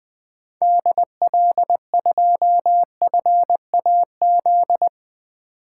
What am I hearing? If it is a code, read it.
Morse code, heard as a keyed tone, about 20 wpm, 700 Hz. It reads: DL2FAZ
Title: 2023 SOTA Activator Call Signs - 20wpm